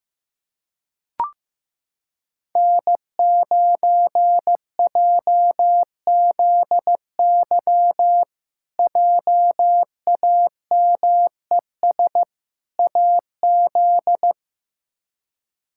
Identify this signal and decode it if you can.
Morse code, heard as a keyed tone, about 16 wpm, 700 Hz. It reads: N9JZY JAMES AZ
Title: K1USN SST QSO Practice - 15wpm